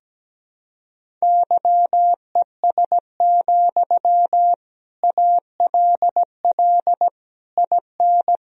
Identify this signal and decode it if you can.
Morse code, heard as a keyed tone, about 17 wpm, 700 Hz. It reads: YES, ALL IN
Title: Sentences from Top 1000 Words - Review Part 1 - 17wpm